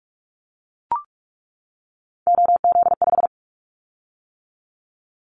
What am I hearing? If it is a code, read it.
Morse code, heard as a keyed tone, about 44 wpm, 700 Hz. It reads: OZ5
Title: Callsign Prefixes for Top 100 DXCC - 45wpm